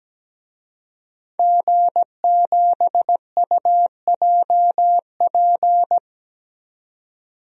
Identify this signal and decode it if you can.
Morse code, heard as a keyed tone, about 17 wpm, 700 Hz. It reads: G7UJP
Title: DXCC-100 Call Signs - 5 Characters - 17wpm